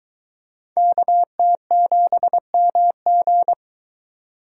Morse code, heard as a keyed tone, about 23 wpm, 700 Hz. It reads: KT7MG